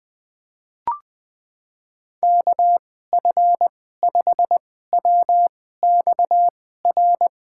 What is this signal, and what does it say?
Morse code, heard as a keyed tone, about 21 wpm, spaced as for 15 wpm, 700 Hz. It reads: KF5WXR
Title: USA Call Signs - 6 Characters - 2x Character Spacing - 20wpm